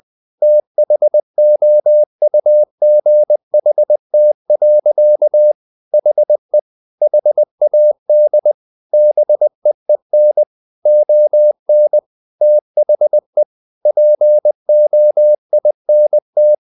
Morse code, heard as a keyed tone, about 20 wpm, 600 Hz. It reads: THOUGHT. HE HAD BEEN ON THE POINT
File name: the old man and the sea part 2 20wpm